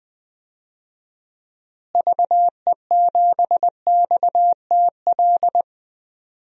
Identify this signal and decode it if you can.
Morse code, heard as a keyed tone, about 20 wpm, 700 Hz. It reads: VE7XTL
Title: DXCC-100 Call Signs - 6 Characters - Repeated 3x - 20wpm